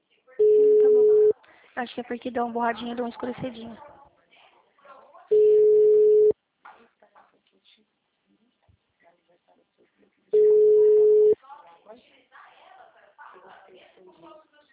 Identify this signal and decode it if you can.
Ringback tone (Europe, 425 Hz), with speech.
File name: out- -2021- - - .64611